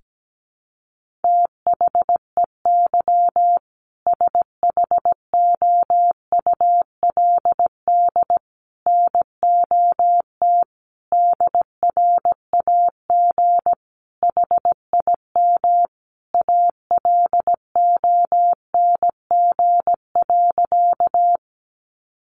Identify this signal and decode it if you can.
Morse code, heard as a keyed tone, about 17 wpm, 700 Hz. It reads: THEY SHOULD NOT DRAG HIM ALONG.
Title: Sentences from Top 1500 Words 17wpm